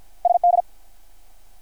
Morse code, heard as a keyed tone, about 51 wpm, 700 Hz. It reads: SD
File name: archived / CRSnet